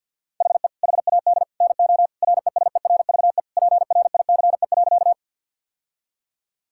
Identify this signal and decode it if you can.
Morse code, heard as a keyed tone, about 52 wpm, 700 Hz. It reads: HE HAD NO RESERVE PRICE.
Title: Sentences from Top 1000 Words - Review Part 6 - 50wpm